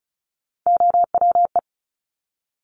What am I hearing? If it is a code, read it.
Morse code, heard as a keyed tone, about 36 wpm, 700 Hz. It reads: OWE
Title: Top Three Letter Combinations in Words 35wpm